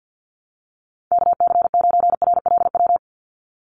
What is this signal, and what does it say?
Morse code, heard as a keyed tone, about 50 wpm, 700 Hz. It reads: KC9RLW
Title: USA Call Signs 50wpm